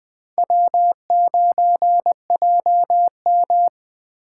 Morse code, heard as a keyed tone, about 20 wpm, 700 Hz. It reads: W9JM
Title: Speed Racing - USA 1x2 Call Signs 17wpm